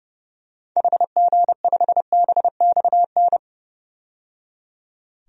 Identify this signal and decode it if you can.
Morse code, heard as a keyed tone, about 30 wpm, 700 Hz. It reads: HG5BXN
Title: DXCC-100 Call Signs - 6 Characters - 30wpm